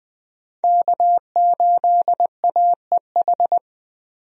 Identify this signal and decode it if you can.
Morse code, heard as a keyed tone, about 20 wpm, 700 Hz. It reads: K8AEH
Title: USA Call Signs - 5 Characters - 20wpm